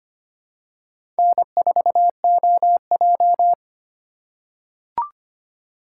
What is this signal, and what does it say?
Morse code, heard as a keyed tone, about 25 wpm, 700 Hz. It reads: N4OJ
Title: USA Call Signs - 3x Character Spacing - 25wpm